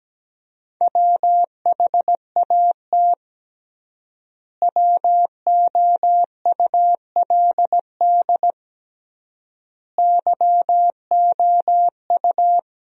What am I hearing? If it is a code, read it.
Morse code, heard as a keyed tone, about 17 wpm, 700 Hz. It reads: WHAT WOULD YOU
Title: 3x Word-Spacing - Sentences from Top 300 Words - 17wpm